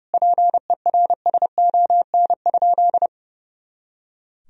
Morse code, heard as a keyed tone, about 30 wpm, 700 Hz. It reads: PERSON?
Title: Sentences from Top 400 Words and Abbreviations 30wpm